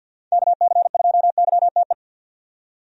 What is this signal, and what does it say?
Morse code, heard as a keyed tone, about 46 wpm, 700 Hz. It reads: KC2YTE
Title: USA Call Signs - 6 Characters - 50wpm